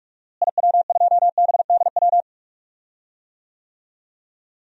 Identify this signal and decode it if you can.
Morse code, heard as a keyed tone, about 46 wpm, 700 Hz. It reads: IW2BDW
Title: DXCC-100 Call Signs - 6 Characters - Repeated 3x - 45wpm